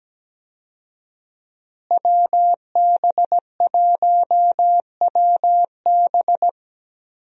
Morse code, heard as a keyed tone, about 17 wpm, 700 Hz. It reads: WB1WB
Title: USA Call Signs - 5 Characters - Repeated 2x - 17wpm